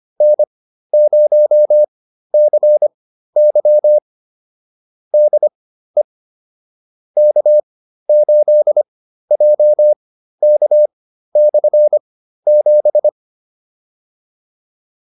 Morse code, heard as a keyed tone, about 25 wpm, spaced as for 13 wpm, 600 Hz. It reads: N0CY DE K8JK/7
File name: QSO310 13